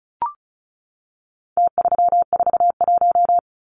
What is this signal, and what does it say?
Morse code, heard as a keyed tone, about 34 wpm, 700 Hz. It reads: T341
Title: Numbers - Contest Style - 4-Digit 35wpm